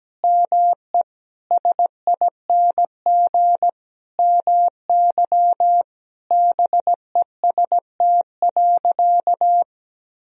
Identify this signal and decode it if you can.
Morse code, heard as a keyed tone, about 17 wpm, 700 Hz. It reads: ME SING MY BEST.
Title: Sentences from Top 1000 Words - Review Part 2 - 17wpm